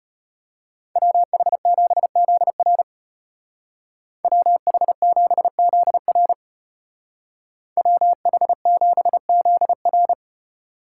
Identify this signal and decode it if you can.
Morse code, heard as a keyed tone, about 30 wpm, 700 Hz. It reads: WH7ZR WH7ZR WH7ZR
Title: Speed Racing - USA 2x2 Call Signs 25wpm